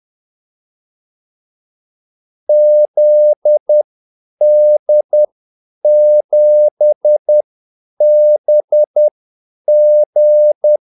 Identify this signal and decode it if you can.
Morse code, heard as a keyed tone, about 10 wpm, spaced as for 8 wpm, 600 Hz. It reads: ZD7BG
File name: Callsigns-10-8wpm